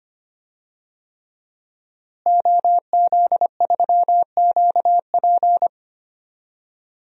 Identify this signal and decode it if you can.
Morse code, heard as a keyed tone, about 25 wpm, 700 Hz. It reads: OZ3QP
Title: DXCC-100 Call Signs - 5 Characters - Repeated 3x - 25wpm